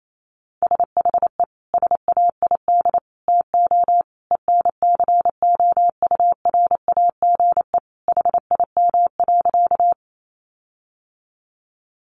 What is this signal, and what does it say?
Morse code, heard as a keyed tone, about 28 wpm, 700 Hz. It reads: SHE SAID TO ENCOURAGE HIM.
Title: Sentences from Top 1400 Words 28wpm